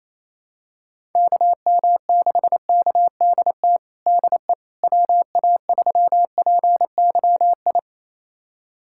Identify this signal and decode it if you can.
Morse code, heard as a keyed tone, about 27 wpm, 700 Hz. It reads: KM6KDT DE WA3PYI
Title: USA Callsign DE Callsign - 6 Characters - 28wpm